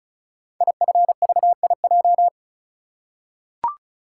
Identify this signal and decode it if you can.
Morse code, heard as a keyed tone, about 36 wpm, 700 Hz. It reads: IFVIJ
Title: Code Groups - 5 Letters - 35wpm